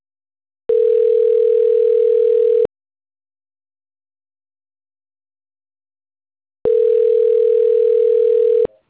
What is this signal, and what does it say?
Ringback tone (North America), 440 + 480 Hz.